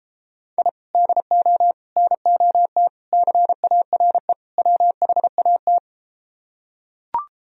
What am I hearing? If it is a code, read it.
Morse code, heard as a keyed tone, about 33 wpm, 700 Hz. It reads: I DO NOT CARE WHAT